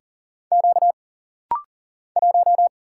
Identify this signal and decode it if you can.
Morse code, heard as a keyed tone, about 40 wpm, 700 Hz. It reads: Q 1
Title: Single Letter-Number - Rapid-Fire 40wpm